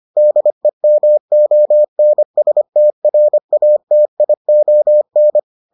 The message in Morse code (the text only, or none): DEMONSTRATION